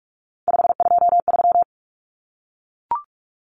Morse code, heard as a keyed tone, about 44 wpm, 700 Hz. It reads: 523
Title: Numbers - 3-Digit 45wpm